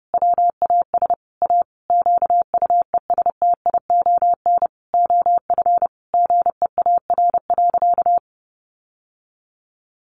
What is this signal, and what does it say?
Morse code, heard as a keyed tone, about 30 wpm, 700 Hz. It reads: WAS A QUESTION OF GEAR.
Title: Sentences from Top 1400 Words 30wpm